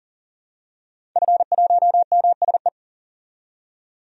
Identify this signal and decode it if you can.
Morse code, heard as a keyed tone, about 40 wpm, 700 Hz. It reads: F1MSE
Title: DXCC-100 Call Signs - 5 Characters - 40wpm